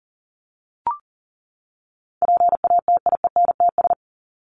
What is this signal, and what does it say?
Morse code, heard as a keyed tone, about 40 wpm, 700 Hz. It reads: PATIENTS
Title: ES:S Suffix - Top 100 Words with Base Words 40wpm